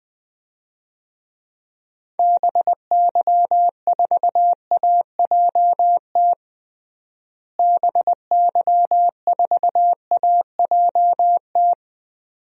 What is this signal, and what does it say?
Morse code, heard as a keyed tone, about 20 wpm, 700 Hz. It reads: BY4AJT BY4AJT
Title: DXCC-100 Call Signs - 6 Characters - Repeated 2x - 20wpm